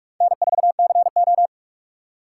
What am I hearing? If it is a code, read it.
Morse code, heard as a keyed tone, about 54 wpm, 700 Hz. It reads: N4CO
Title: Speed Racing - USA Call Signs 45wpm